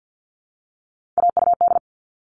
Morse code, heard as a keyed tone, about 48 wpm, 700 Hz. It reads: AUD
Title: All Cut Numbers - 3-Digit 50wpm